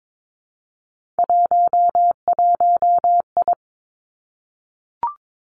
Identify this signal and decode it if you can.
Morse code, heard as a keyed tone, about 22 wpm, 700 Hz. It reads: 11I
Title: Three Challenging Letter-Number Combinations 22wpm